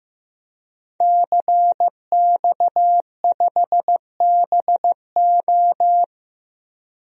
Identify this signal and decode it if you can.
Morse code, heard as a keyed tone, about 15 wpm, 700 Hz. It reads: CX5BO
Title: DXCC-100 Call Signs - 5 Characters - Repeated 3x - 15wpm